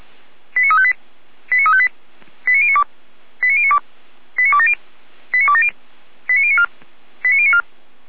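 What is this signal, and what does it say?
Five-tone selective call (ZVEI), heard as quick runs of five tones, tones of about 70 ms. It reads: 89358 89358 89092 89092 89270 89270 89094 89094
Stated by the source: ZVEI five-tone selective call 89262 received on SDR, fire brigade Feuerwehr Orlamünde.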